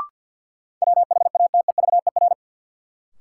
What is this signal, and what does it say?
Morse code, heard as a keyed tone, about 53 wpm, 700 Hz. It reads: WHATEVER
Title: Sentences from Top 1100 Words - New Word Drill Alt - 50wpm